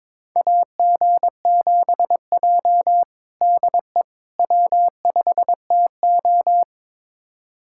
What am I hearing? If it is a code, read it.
Morse code, heard as a keyed tone, about 22 wpm, 700 Hz. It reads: AG7J DE W5TO
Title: USA Callsign DE Callsign - 4 Characters - 22wpm